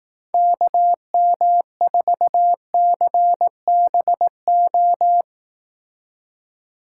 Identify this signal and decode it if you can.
Morse code, heard as a keyed tone, about 18 wpm, 700 Hz. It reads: KM4CBO
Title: Speed Racing - USA 2x3 Call Signs 15wpm